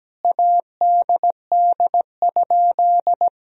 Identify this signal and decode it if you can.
Morse code, heard as a keyed tone, about 17 wpm, 700 Hz. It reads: ADD?
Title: Sentences from Top 1000 Words - Review Part 1 - 17wpm